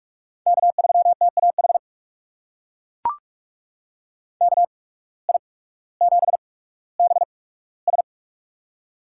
Morse code, heard as a keyed tone, about 43 wpm, spaced as for 19 wpm, 700 Hz. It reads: K3TAH KI7BS
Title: USA Call Signs - 5 Characters - 8x Character Spacing - 45wpm